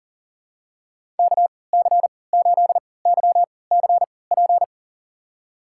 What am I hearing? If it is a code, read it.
Morse code, heard as a keyed tone, about 37 wpm, spaced as for 22 wpm, 700 Hz. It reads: KC8YCP
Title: USA Call Signs - 6 Characters - 3x Character Spacing - 40wpm